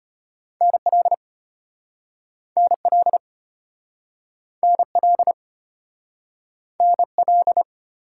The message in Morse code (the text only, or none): NL NL NL NL